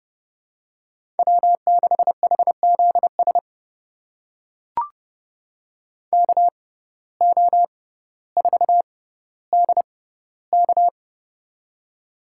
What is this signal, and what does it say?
Morse code, heard as a keyed tone, about 31 wpm, spaced as for 16 wpm, 700 Hz. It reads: W6HZS KO4DK
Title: USA Call Signs - 5 Characters - 6x Character Spacing - 30wpm